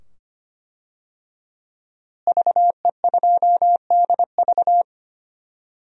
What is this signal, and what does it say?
Morse code, heard as a keyed tone, about 25 wpm, 700 Hz. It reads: VE2DV
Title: DXCC-100 Call Signs - 5 Characters - Repeated 2x - 25wpm